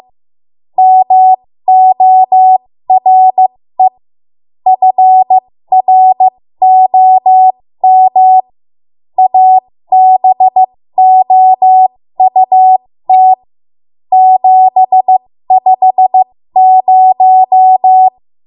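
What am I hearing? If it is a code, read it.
Morse code, heard as a keyed tone, about 15 wpm, spaced as for 13 wpm, 750 Hz. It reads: MORE FROM ABOUT 750
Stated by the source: ARRL Morse code practice at 13 WPM.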